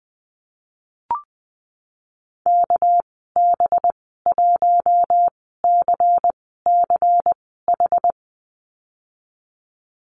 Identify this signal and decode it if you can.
Morse code, heard as a keyed tone, about 20 wpm, spaced as for 15 wpm, 700 Hz. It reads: KB1CCH
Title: 207 Lesson 33g - Letter j - Call Signs 20wpm